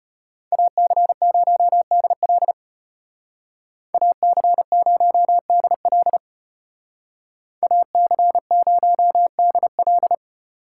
Morse code, heard as a keyed tone, about 35 wpm, 700 Hz. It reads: AC0DL AC0DL AC0DL